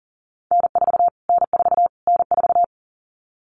Morse code, heard as a keyed tone, about 40 wpm, 700 Hz. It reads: N4 N4 N4
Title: Callsign Prefixes for Top 25 DXCC - Repeated 3x - 40wpm